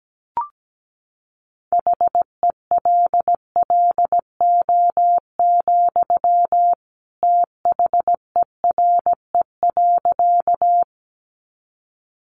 Morse code, heard as a keyed tone, about 17 wpm, 700 Hz. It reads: HELLO, THERE.
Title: Sentences from Top 1100 Words 17wpm